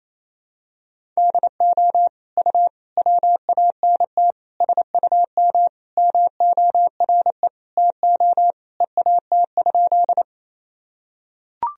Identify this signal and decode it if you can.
Morse code, heard as a keyed tone, about 28 wpm, 700 Hz. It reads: DO U WANT SUM MORE TO EAT?
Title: Sentences from Top 300 Words and Abbreviations 28wpm